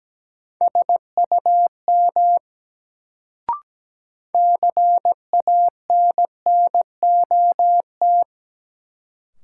Morse code, heard as a keyed tone, about 17 wpm, 700 Hz. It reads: SUM CANNOT
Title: Sentences from Top 1000 Words - Review Part 6 - New Word Drill - 17wpm